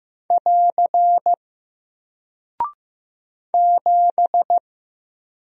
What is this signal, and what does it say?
Morse code, heard as a keyed tone, about 15 wpm, 700 Hz. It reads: <AR> 7
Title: Single Character (Letter:Number:Punctuation:Prosign) Alt 15wpm